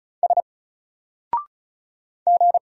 Morse code, heard as a keyed tone, about 34 wpm, 700 Hz. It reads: S G
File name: Single-Letter-Number-35wpm